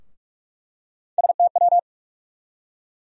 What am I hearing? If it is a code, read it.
Morse code, heard as a keyed tone, about 44 wpm, 700 Hz. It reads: STW
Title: USA Call Sign Postfixes - 3 Letters 45wpm